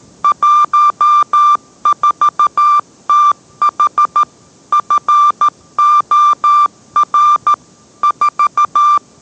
Morse code, heard as a keyed tone, about 13 wpm, 1180 Hz. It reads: 14TH FOR 4